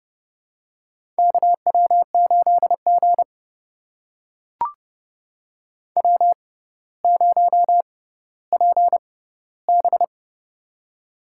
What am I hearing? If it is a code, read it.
Morse code, heard as a keyed tone, about 30 wpm, spaced as for 15 wpm, 700 Hz. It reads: KW8G W0PB
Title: USA Call Signs - 4 Characters - 6x Character Spacing - 30wpm